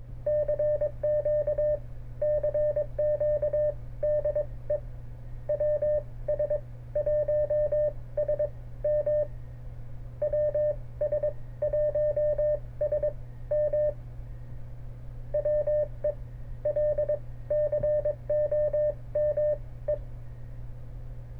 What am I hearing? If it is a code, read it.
Morse code, heard as a keyed tone, about 22 wpm, spaced as for 17 wpm, 600 Hz, with radio noise. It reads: CQ CQ DE WS1SM WS1SM WELCOME